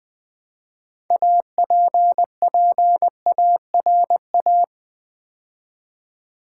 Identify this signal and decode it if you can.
Morse code, heard as a keyed tone, about 20 wpm, 700 Hz. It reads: APPARA
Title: Build A Word - 3 Letters At A Time - 9-Letter Words - 20wpm